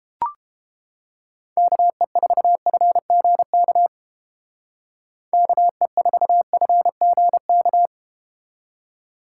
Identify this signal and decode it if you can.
Morse code, heard as a keyed tone, about 31 wpm, 700 Hz. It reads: KE4FGK KE4FGK